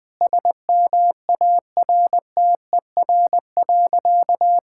Morse code, heard as a keyed tone, about 20 wpm, 700 Hz. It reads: SMARTER.
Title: All-Age Appropriate Jokes - 20wpm